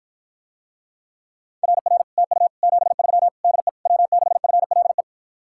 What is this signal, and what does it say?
Morse code, heard as a keyed tone, about 53 wpm, 700 Hz. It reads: RR TU 73 DE W6FLE